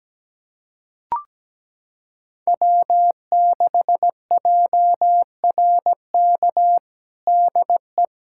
Morse code, heard as a keyed tone, about 17 wpm, 700 Hz. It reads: W6JRK DE
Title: USA Callsign DE Callsign - 5 Characters - 17wpm